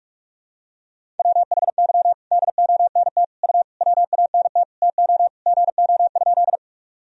Morse code, heard as a keyed tone, about 44 wpm, 700 Hz. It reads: WHY DONT U WANT TO GO?
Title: Sentences from Top 300 Words and Abbreviations 45wpm